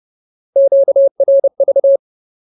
Morse code, heard as a keyed tone, about 30 wpm, 550 Hz. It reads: QRV
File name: Ultimate Word List / 30WPM Ultimate